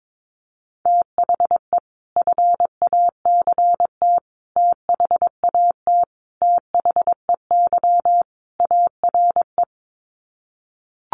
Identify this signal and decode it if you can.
Morse code, heard as a keyed tone, about 22 wpm, 700 Hz. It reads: THE FACT THAT THEY ARE
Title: Sets of 5 Words Encore2 22wpm